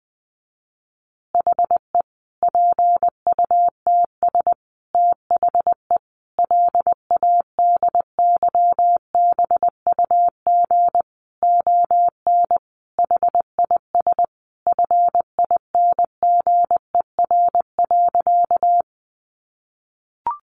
Morse code, heard as a keyed tone, about 20 wpm, 700 Hz. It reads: HE PUTS THE LADYBUG ON HIS FINGER.
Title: Additional Easy Phrases 20wpm